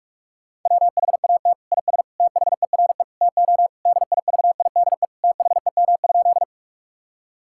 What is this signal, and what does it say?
Morse code, heard as a keyed tone, about 46 wpm, 700 Hz. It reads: WHAT IS THERE TO DIVIDE THEM?
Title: Sentences from Top 1600 Words 45wpm